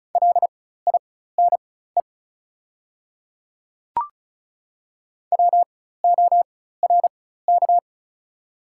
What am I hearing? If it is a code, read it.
Morse code, heard as a keyed tone, about 34 wpm, spaced as for 16 wpm, 700 Hz. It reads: LINE WORK
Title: 4x Character Spacing - Top 200 Words - 35wpm